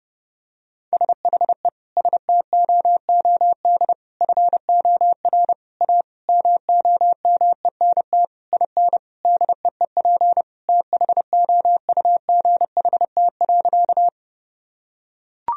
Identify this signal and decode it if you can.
Morse code, heard as a keyed tone, about 30 wpm, 700 Hz. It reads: SHE STOOD FOR A MOMENT IN DEEP THOUGHT.
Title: Sentences from Top 1000 Words - Review Part 3 - 30wpm